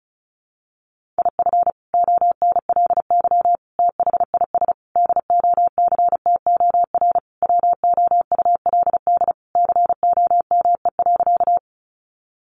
Morse code, heard as a keyed tone, about 34 wpm, 700 Hz. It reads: IF ONLY THIS DOCTOR WOULD COME.
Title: Sentences from Top 1500 Words 35wpm